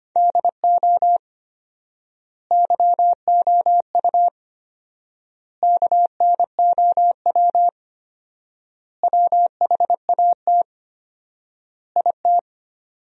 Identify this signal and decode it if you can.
Morse code, heard as a keyed tone, about 25 wpm, 700 Hz. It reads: DO YOU KNOW WHAT IT
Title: 4x Word-Spacing - Sentences from Top 100 Words - 25wpm